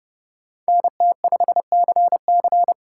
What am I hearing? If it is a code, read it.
Morse code, heard as a keyed tone, about 31 wpm, 700 Hz. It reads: NT5CC